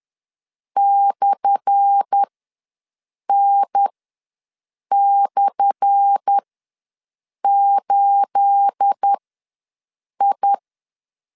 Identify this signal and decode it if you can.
Morse code, heard as a keyed tone, about 10 wpm, spaced as for 6 wpm, 800 Hz. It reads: /N/8I
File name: LEZIONI / 10 SIGNS / 10A SLASH ws 10wpm 4121